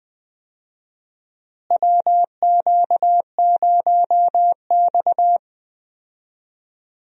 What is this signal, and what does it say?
Morse code, heard as a keyed tone, about 20 wpm, 700 Hz. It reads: WQ0X
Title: USA Call Signs - 4 Characters - Repeated 3x - 20wpm